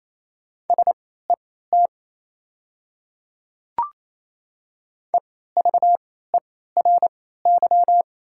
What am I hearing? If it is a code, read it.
Morse code, heard as a keyed tone, about 28 wpm, spaced as for 16 wpm, 700 Hz. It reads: SET EVERY
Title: 3x Character Spacing - Top 200 Words - 28wpm